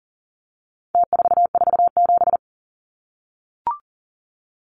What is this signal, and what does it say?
Morse code, heard as a keyed tone, about 40 wpm, 700 Hz. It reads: T447
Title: Numbers - Contest Style - 4-Digit 40wpm